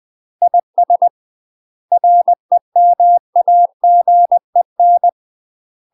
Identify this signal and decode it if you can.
Morse code, heard as a keyed tone, about 20 wpm, 700 Hz. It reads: IS REMAGEN
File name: UK MASTER QSOs TRAINING / 20WPM / UK MASTER QSOs 0065